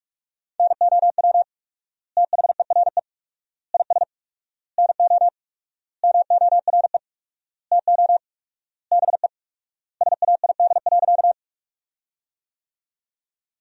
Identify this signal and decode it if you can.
Morse code, heard as a keyed tone, about 46 wpm, 700 Hz. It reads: NOW THERE IS NO MORE TO BE SAID.
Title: 4x Word-Spacing - Sentences from Top 100 Words - 45wpm